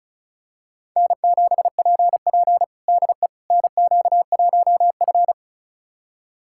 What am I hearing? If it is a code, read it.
Morse code, heard as a keyed tone, about 36 wpm, 700 Hz. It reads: N7PP DE NQ1F
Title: USA Callsign DE Callsign - 4 Characters - 35wpm